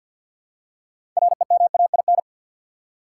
Morse code, heard as a keyed tone, about 53 wpm, 700 Hz. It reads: REMAIN
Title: Six Letter Words - 50wpm